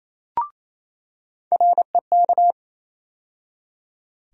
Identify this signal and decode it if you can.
Morse code, heard as a keyed tone, about 28 wpm, 700 Hz. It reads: REK
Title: USA Call Sign Postfixes - 3 Letters 28wpm